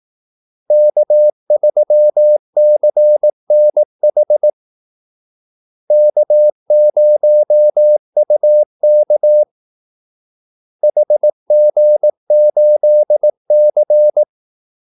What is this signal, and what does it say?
Morse code, heard as a keyed tone, about 18 wpm, 600 Hz. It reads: K3CNH K0UK HG8C
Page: Kaldesignaler 18 wpm